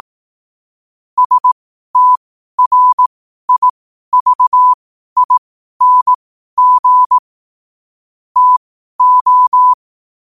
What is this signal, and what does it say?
Morse code, heard as a keyed tone, about 18 wpm, spaced as for 12 wpm, 1000 Hz. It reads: STRIVING TO
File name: qotdfiles / 12wpm